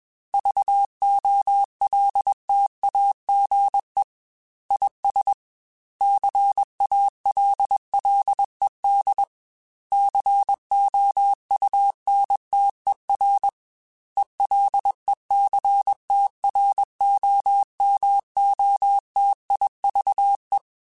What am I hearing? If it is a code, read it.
Morse code, heard as a keyed tone, about 21 wpm, 800 Hz. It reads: VOLTAGE IS CALLED COUNTER ELECTROMOTIVE